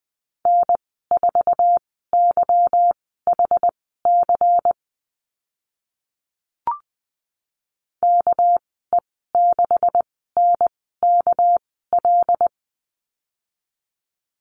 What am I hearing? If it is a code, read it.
Morse code, heard as a keyed tone, about 20 wpm, spaced as for 14 wpm, 700 Hz. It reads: N4YHC KE6NKL
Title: 207 Lesson 33g - Letter j - Call Signs 20wpm